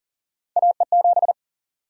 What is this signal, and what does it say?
Morse code, heard as a keyed tone, about 40 wpm, 700 Hz. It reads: AE7
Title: USA Call Sign Prefixes - 40wpm